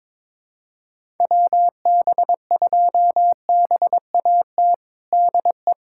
Morse code, heard as a keyed tone, about 22 wpm, 700 Hz. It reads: WB2BAT DE
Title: USA Callsign DE Callsign - 6 Characters - 22wpm